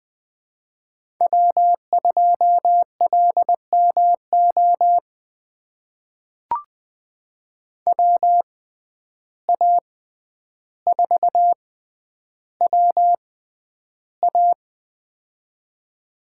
Morse code, heard as a keyed tone, about 20 wpm, spaced as for 10 wpm, 700 Hz. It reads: W2LMO WA4WA
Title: USA Call Signs - 5 Characters - 6x Character Spacing - 20wpm